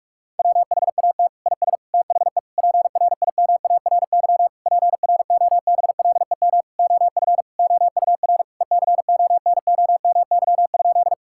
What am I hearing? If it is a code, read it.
Morse code, heard as a keyed tone, about 46 wpm, 700 Hz. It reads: WHAT IS THE PRIMARY PROBLEM OF OUR ECONOMY?
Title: Sentences from Top 1000 Words - Review Part 5 - 45wpm